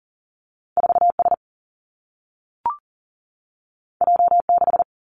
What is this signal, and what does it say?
Morse code, heard as a keyed tone, about 39 wpm, 700 Hz. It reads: 4 S J 6
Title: Two Challenging Letter-Number Combinations 40wpm